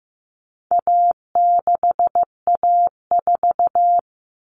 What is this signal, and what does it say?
Morse code, heard as a keyed tone, about 15 wpm, 700 Hz. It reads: A6A4
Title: Common Cut Numbers - 4-Digit 15wpm